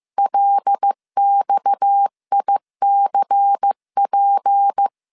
Morse code, heard as a keyed tone, about 15 wpm, 800 Hz. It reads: LXICP